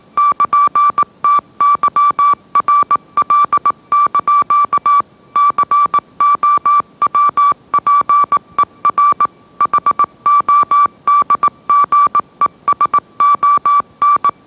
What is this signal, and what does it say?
Morse code, heard as a keyed tone, about 18 wpm, 1160 Hz. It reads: <KN>TYRL) COWPER HODGESON
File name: mrx / mrx082 20wpm